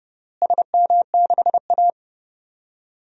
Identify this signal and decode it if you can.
Morse code, heard as a keyed tone, about 30 wpm, 700 Hz. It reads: SM6A
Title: DXCC-100 Call Signs - 4 Characters - Repeated 3x - 30wpm